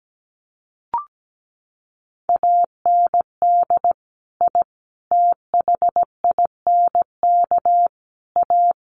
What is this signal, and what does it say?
Morse code, heard as a keyed tone, about 17 wpm, 700 Hz. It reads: AND I THINK A
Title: Sets of 5 Words Encore2 17wpm